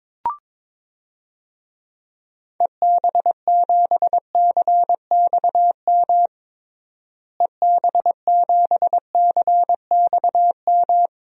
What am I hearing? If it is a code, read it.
Morse code, heard as a keyed tone, about 22 wpm, 700 Hz. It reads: EB7CXM EB7CXM
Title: DXCC-100 Call Signs - 6 Characters - Repeated 2x - 22wpm